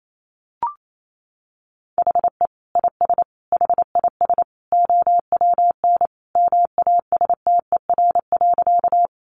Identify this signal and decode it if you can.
Morse code, heard as a keyed tone, about 28 wpm, 700 Hz. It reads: HE IS HIS OWN MASTER.
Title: Sentences from Top 1300 Words 28wpm